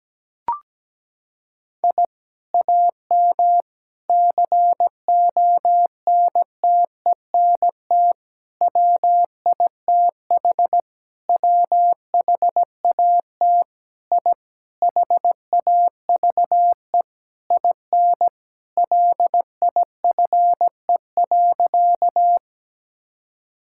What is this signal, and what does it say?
Morse code, heard as a keyed tone, about 17 wpm, 700 Hz. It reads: I AM CONTENT WITH WHAT I HAVE IN LIFE.
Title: Sentences from Top 1000 Words - Review Part 6 - 17wpm